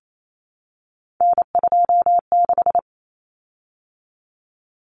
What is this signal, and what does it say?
Morse code, heard as a keyed tone, about 28 wpm, 700 Hz. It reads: N26
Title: Common Cut Numbers - 3-Digit 28wpm